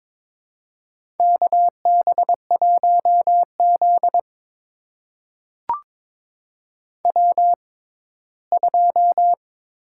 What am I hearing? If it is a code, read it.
Morse code, heard as a keyed tone, about 22 wpm, spaced as for 15 wpm, 700 Hz. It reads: KB1Z W2
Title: USA Call Signs - 4 Characters - 6x Character Spacing - 22wpm